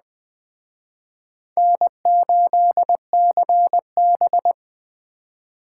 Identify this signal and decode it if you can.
Morse code, heard as a keyed tone, about 21 wpm, 700 Hz. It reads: N8CB